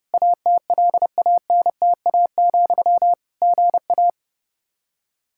Morse code, heard as a keyed tone, about 30 wpm, 700 Hz. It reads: ATLANTA, GA
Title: Practice QSO - Exchange 1 - Beginning - 30wpm